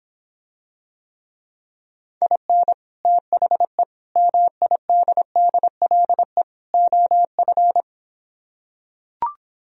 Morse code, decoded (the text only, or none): IN THE MIDDLE OF